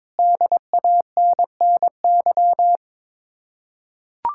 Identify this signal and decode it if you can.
Morse code, heard as a keyed tone, about 22 wpm, 700 Hz. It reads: DANNY
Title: Practice QSO - Exchange 1 - Beginning - 22wpm